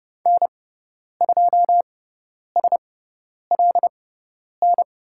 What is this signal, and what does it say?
Morse code, heard as a keyed tone, about 30 wpm, spaced as for 10 wpm, 700 Hz. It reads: N2SLN
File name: Lesson 18g - Letter m - Call Signs 30wpm